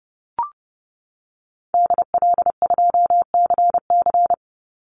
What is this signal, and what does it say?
Morse code, heard as a keyed tone, about 31 wpm, 700 Hz. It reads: DL2CC
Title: Call Signs - 30wpm